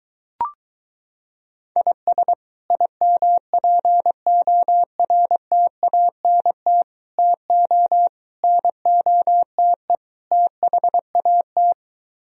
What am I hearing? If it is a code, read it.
Morse code, heard as a keyed tone, about 23 wpm, 700 Hz. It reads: IS IMPORTANT TO NOTE THAT